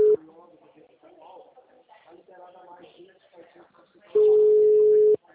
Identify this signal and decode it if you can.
Ringback tone (Europe, 425 Hz), with speech.
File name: out- -2016- - - .51170